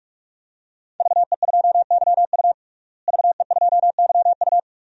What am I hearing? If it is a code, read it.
Morse code, heard as a keyed tone, about 46 wpm, 700 Hz. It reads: VE2YU VE2YU
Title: DXCC-100 Call Signs - 5 Characters - Repeated 2x - 45wpm